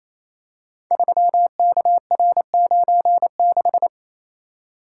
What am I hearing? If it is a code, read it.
Morse code, heard as a keyed tone, about 27 wpm, 700 Hz. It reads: 3KR96
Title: Code Groups - 5 Letter:Numbers - 28wpm